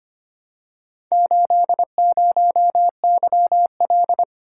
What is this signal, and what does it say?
Morse code, heard as a keyed tone, about 25 wpm, 700 Hz. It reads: 80YL
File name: practice / 25wpm / Four Letter-Number Combinations - 25wpm